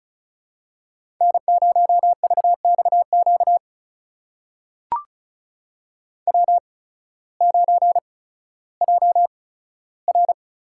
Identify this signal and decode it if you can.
Morse code, heard as a keyed tone, about 36 wpm, spaced as for 16 wpm, 700 Hz. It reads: N0VXQ W9JR
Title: USA Call Signs - 5 Characters - 8x Character Spacing - 35wpm